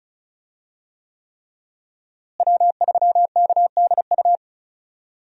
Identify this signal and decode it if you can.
Morse code, heard as a keyed tone, about 36 wpm, 700 Hz. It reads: W3KDU
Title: USA Call Signs - 5 Characters - Repeated 3x - 35wpm